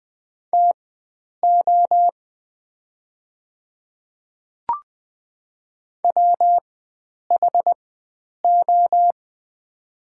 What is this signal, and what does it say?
Morse code, heard as a keyed tone, about 20 wpm, spaced as for 9 wpm, 690 Hz. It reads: TO WHO
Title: 4x Character Spacing - Top 100 Words - 20wpm